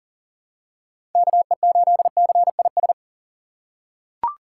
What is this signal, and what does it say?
Morse code, heard as a keyed tone, about 37 wpm, 700 Hz. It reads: KE8CIS
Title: USA Call Signs - 6 Characters - 2x Character Spacing - 40wpm